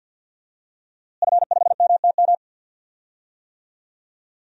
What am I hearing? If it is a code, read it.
Morse code, heard as a keyed tone, about 52 wpm, 700 Hz. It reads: F5MTM
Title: DXCC-100 Call Signs - 5 Characters - Repeated 3x - 50wpm